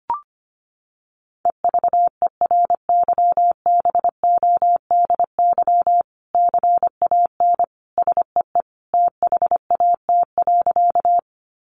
Morse code, heard as a keyed tone, about 25 wpm, 700 Hz. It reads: EVERYBODY CAN SEE THAT.
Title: Sentences from Top 1400 Words 25wpm